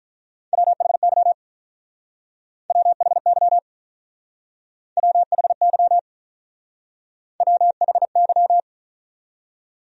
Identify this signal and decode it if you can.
Morse code, heard as a keyed tone, about 45 wpm, 700 Hz. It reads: WHY WHY WHY WHY